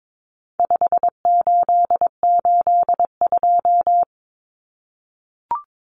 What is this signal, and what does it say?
Morse code, heard as a keyed tone, about 22 wpm, 700 Hz. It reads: 5882
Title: Numbers - 4-Digit 22wpm